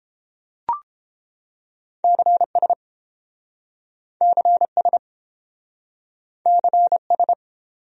Morse code, heard as a keyed tone, about 28 wpm, 700 Hz. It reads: CS CS CS